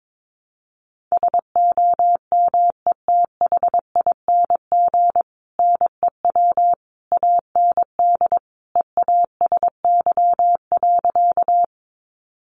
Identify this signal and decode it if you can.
Morse code, heard as a keyed tone, about 22 wpm, 700 Hz. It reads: SOMETHING NEW AND EASY.
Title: Sentences from Top 700 Words 22wpm